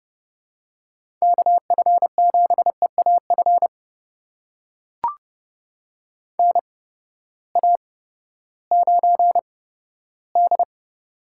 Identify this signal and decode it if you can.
Morse code, heard as a keyed tone, about 30 wpm, spaced as for 15 wpm, 700 Hz. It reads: KF7EAF NA9D
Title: USA Call Signs - 8x Character Spacing - 30wpm